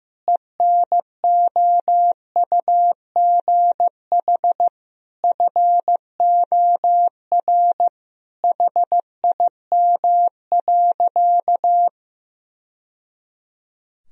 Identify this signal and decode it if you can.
Morse code, heard as a keyed tone, about 15 wpm, 700 Hz. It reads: ENOUGH FOR HIM.
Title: Sentences from Top 1000 Words - Review Part 4 - 15wpm